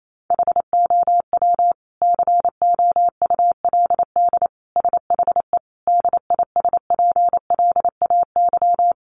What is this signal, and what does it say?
Morse code, heard as a keyed tone, about 28 wpm, 700 Hz. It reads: HOW COULD SHE DISPLAY
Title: Sentences from Top 1200 Words 28wpm